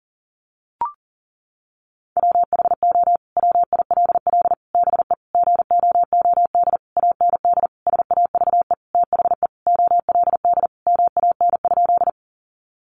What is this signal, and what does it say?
Morse code, heard as a keyed tone, about 40 wpm, 700 Hz. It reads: WHO WILL BE GOOD AND SAVE THE OLD MAN?
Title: Sentences from Top 1100 Words 40wpm